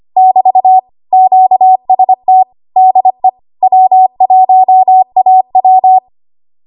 Morse code, heard as a keyed tone, about 25 wpm, 750 Hz. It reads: <BT> QST DE W1AW
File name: Archive / 25 WPM / 25WPM